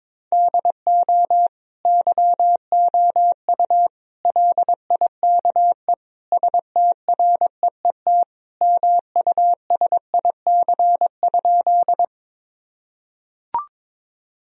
Morse code, heard as a keyed tone, about 22 wpm, 700 Hz. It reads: DO YOU LIKE STREET MUSIC?
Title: Sentences from Top 1000 Words - Review Part 3 - 22wpm